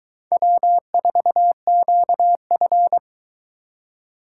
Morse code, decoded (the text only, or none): W4QF